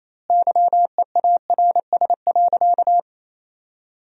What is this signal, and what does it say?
Morse code, heard as a keyed tone, about 28 wpm, 700 Hz. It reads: YEARS.
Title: 2x Word-Spacing - Sentences from Top 200 Words - 28wpm